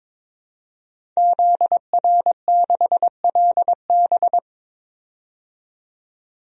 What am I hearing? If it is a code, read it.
Morse code, heard as a keyed tone, about 21 wpm, 700 Hz. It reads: ZR6LB
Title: DXCC-100 Call Signs - 5 Characters - 22wpm